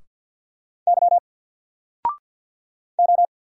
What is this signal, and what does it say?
Morse code, heard as a keyed tone, about 50 wpm, 700 Hz. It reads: Y O
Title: Single Character (Letter:Number:Punctuation:Prosign) 50wpm